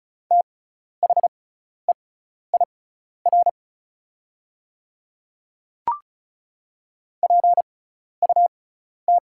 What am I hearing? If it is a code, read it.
Morse code, heard as a keyed tone, about 36 wpm, spaced as for 12 wpm, 700 Hz. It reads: THEIR PUT
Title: 6x Character Spacing - Top 200 Words - 35wpm